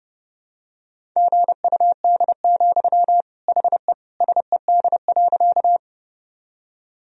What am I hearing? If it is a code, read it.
Morse code, heard as a keyed tone, about 30 wpm, 700 Hz. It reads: GUD, HE SED.
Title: Sentences from Top 200 Words and Abbreviations 30wpm